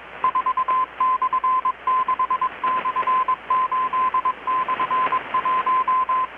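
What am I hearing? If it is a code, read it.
Morse code, heard as a keyed tone, about 21 wpm, 1030 Hz, with radio noise. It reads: 4/6/8/1
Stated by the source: Morse from KPH's Night of Nights XX maritime radiotelegraphy event, received in Chiapas, Mexico about 3,800 km away on a Kenwood R-600 with a grounded 10 m long-wire antenna.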